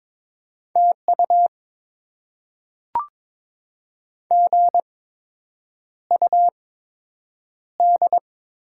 Morse code, heard as a keyed tone, about 22 wpm, spaced as for 8 wpm, 700 Hz. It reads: TU GUD
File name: Top 20 QSO Elements - 8x Character Spacing - 22wpm